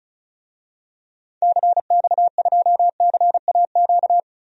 Morse code, heard as a keyed tone, about 36 wpm, 700 Hz. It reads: CX2CAQ
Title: DXCC-100 Call Signs - 6 Characters - 35wpm